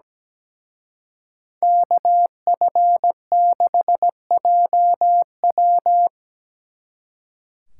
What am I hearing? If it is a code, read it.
Morse code, heard as a keyed tone, about 20 wpm, 700 Hz. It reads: KF6JW
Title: Speed Racing - USA 2x2 Call Signs 17wpm